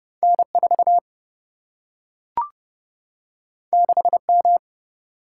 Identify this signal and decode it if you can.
Morse code, heard as a keyed tone, about 29 wpm, 700 Hz. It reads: N4 6M
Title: Single-Letter Single-Number 30wpm